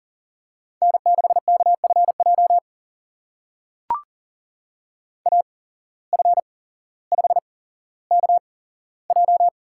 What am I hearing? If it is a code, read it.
Morse code, heard as a keyed tone, about 40 wpm, spaced as for 17 wpm, 700 Hz. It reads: N6KFJ AF5KJ
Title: USA Call Signs - 5 Characters - 8x Character Spacing - 40wpm